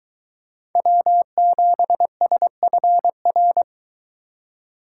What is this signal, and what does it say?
Morse code, heard as a keyed tone, about 23 wpm, 700 Hz. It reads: W7SFR